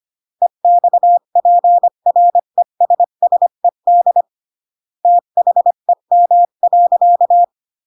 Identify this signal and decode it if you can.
Morse code, heard as a keyed tone, about 25 wpm, 700 Hz. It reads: EXPRESSED THEM.